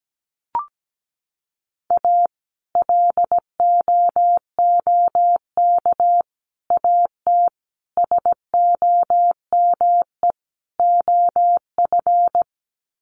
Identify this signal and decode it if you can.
Morse code, heard as a keyed tone, about 17 wpm, 700 Hz. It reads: A LOOK AT SOME OF
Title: Sets of 5 Words Encore2 17wpm